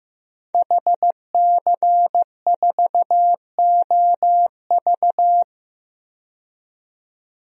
Morse code, heard as a keyed tone, about 15 wpm, 700 Hz. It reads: HC4OV
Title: DXCC-100 Call Signs - 5 Characters - 15wpm